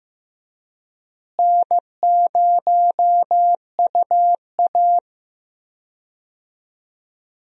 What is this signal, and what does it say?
Morse code, heard as a keyed tone, about 15 wpm, 700 Hz. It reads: N0UA
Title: USA Call Signs - 4 Characters - Repeated 3x - 15wpm